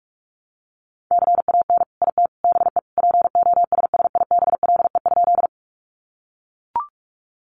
Morse code, heard as a keyed tone, about 44 wpm, 700 Hz. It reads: CAN IT BE POSSIBLE?
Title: Sentences from Top 500 Words 45wpm